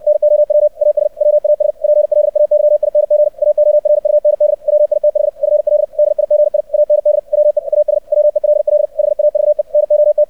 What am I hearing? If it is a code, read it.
Morse code, heard as a keyed tone, about 64 wpm, 600 Hz. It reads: NOW AS WAS PLAYING AORUND WITH FL DIGI AND WE AS WELL HAVE NOT
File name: archived / CRSnet